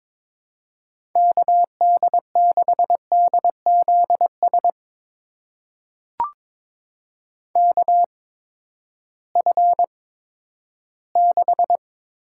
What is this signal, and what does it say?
Morse code, heard as a keyed tone, about 22 wpm, spaced as for 12 wpm, 700 Hz. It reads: KD6DZS KF6
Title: USA Call Signs - 6 Characters - 8x Character Spacing - 22wpm